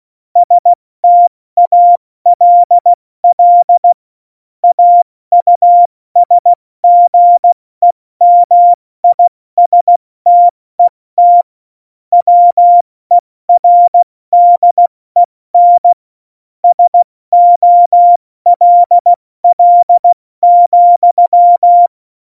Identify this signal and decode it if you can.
Morse code, heard as a keyed tone, about 16 wpm, 700 Hz. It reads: STALL AUSGEMISTET WERDEN SOLL,
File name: Morsekurs / Witze / Witz 03.Dezember 2019, 70 F80